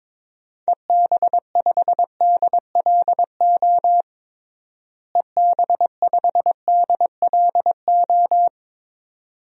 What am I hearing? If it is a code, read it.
Morse code, heard as a keyed tone, about 22 wpm, 700 Hz. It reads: EB5DLO EB5DLO
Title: DXCC-100 Call Signs - 6 Characters - Repeated 2x - 22wpm